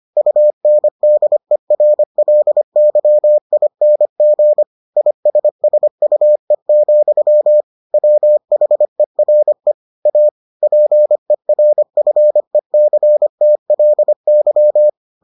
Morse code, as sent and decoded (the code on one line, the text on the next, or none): ..- -. -.. .-. .-.. -.-- .. -. --. / .. ... ... ..- . --..-- / .-- .... .-. / .- / .--. .-. ..-. -.-. - .-.. -.--
UNDERLYING ISSUE, WHERE A PERFECTLY